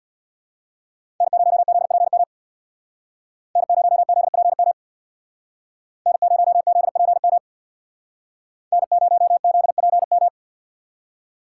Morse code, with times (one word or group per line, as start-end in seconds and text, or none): 1.2-2.2 N0ZPM
3.6-4.7 N0ZPM
6.1-7.4 N0ZPM
8.7-10.3 N0ZPM